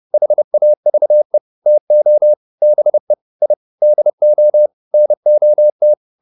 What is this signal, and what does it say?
Morse code, heard as a keyed tone, about 30 wpm, 600 Hz. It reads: HAVE TO BE I DO NOT